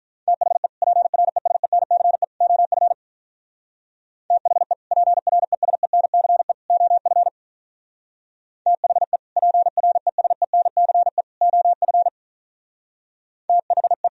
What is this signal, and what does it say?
Morse code, heard as a keyed tone, about 43 wpm, 700 Hz. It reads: THE PRESENCE OF THE PRESENCE OF THE PRESENCE OF THE
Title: Speed-Racing - Sets of 3 Words 35wpm